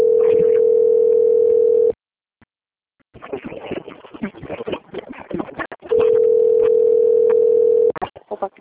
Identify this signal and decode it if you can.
Ringback tone (North America), 440 + 480 Hz, with speech.
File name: out- - - - - .17688